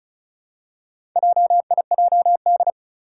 Morse code, heard as a keyed tone, about 34 wpm, 700 Hz. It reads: JIJD
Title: Four Challenging Letter-Number Combinations 35wpm